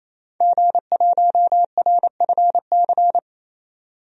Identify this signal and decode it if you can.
Morse code, heard as a keyed tone, about 28 wpm, 700 Hz. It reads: G1RFC
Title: DXCC-100 Call Signs - 5 Characters - 28wpm